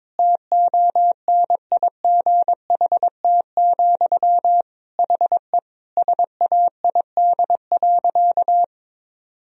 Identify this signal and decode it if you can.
Morse code, heard as a keyed tone, about 22 wpm, 700 Hz. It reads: TONIGHT, HE SAID.
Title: Sentences from Top 1000 Words - Review Part 2 - 22wpm